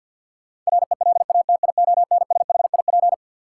Morse code, heard as a keyed tone, about 53 wpm, 700 Hz. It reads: RELATIONSHIP
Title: Sentences from Top 1100 Words - New Word Drill Alt - 50wpm